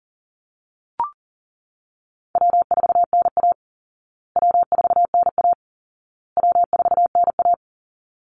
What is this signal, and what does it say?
Morse code, heard as a keyed tone, about 40 wpm, 700 Hz. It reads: W4NA W4NA W4NA
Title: Call Signs Repeated 2x - 40wpm